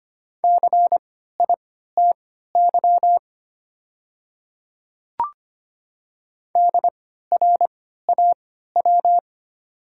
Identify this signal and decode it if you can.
Morse code, heard as a keyed tone, about 25 wpm, spaced as for 14 wpm, 700 Hz. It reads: CITY DRAW
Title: 3x Character Spacing - Top 500 Words - 25wpm